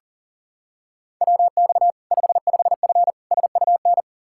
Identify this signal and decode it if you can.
Morse code, heard as a keyed tone, about 40 wpm, 700 Hz. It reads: WX 55F SUN
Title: K1USN SST QSO Practice 40wpm